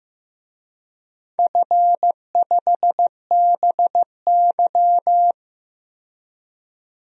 Morse code, heard as a keyed tone, about 15 wpm, 700 Hz. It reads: F5BY
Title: DXCC-100 Call Signs - 4 Characters - 15wpm